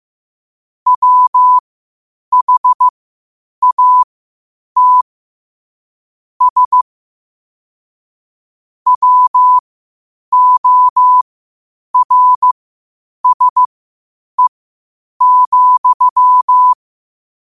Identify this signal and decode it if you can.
Morse code, heard as a keyed tone, about 15 wpm, spaced as for 8 wpm, 1000 Hz. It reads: WHATS WORSE,
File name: qotdfiles / 8wpm